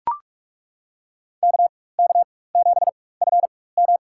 Morse code, heard as a keyed tone, about 42 wpm, spaced as for 21 wpm, 700 Hz. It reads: KK7FM
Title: USA Call Signs - 5 Characters - 4x Character Spacing - 45wpm